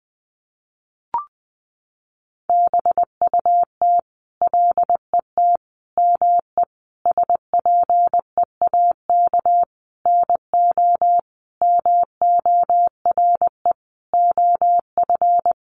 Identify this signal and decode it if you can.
Morse code, heard as a keyed tone, about 20 wpm, 700 Hz. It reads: BUT LET ME SPEAK NO MORE OF
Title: Sentences from Top 1100 Words 20wpm